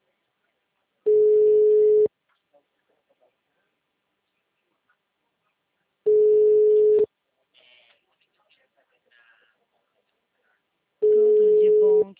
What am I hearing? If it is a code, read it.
Ringback tone (Europe, 425 Hz), with speech.